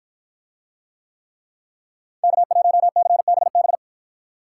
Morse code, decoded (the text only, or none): K1CBB